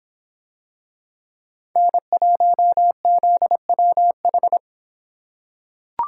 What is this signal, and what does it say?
Morse code, heard as a keyed tone, about 26 wpm, 700 Hz. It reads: N1ZWH